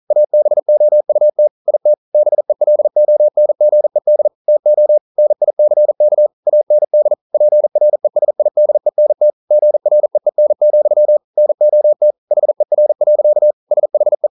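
Morse code, heard as a keyed tone, about 42 wpm, 600 Hz. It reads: ABOUT IT BELONGED TO NICK AND PRESIDENT GREEN, NOT HER. SHE
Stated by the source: Morse code practice at 41 wpm, from CW med Gnister.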